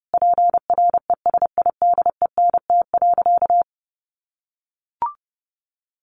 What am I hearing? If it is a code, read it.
Morse code, heard as a keyed tone, about 30 wpm, 700 Hz. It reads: PRESIDENT.
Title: Sentences from Top 1500 Words 30wpm